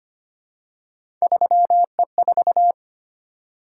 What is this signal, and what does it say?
Morse code, heard as a keyed tone, about 25 wpm, 700 Hz. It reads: 3E4
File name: Three Letter-Number Combinations - 25wpm